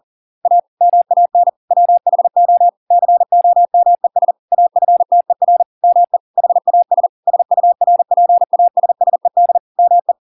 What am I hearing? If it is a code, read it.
Morse code, heard as a keyed tone, about 40 wpm, 700 Hz. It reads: A MAN WHO COMES AFTER ME HAS SURPASSED ME